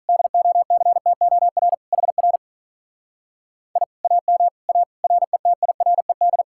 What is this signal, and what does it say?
Morse code, heard as a keyed tone, about 44 wpm, 700 Hz. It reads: DOCTOR HR I AM A RETIRED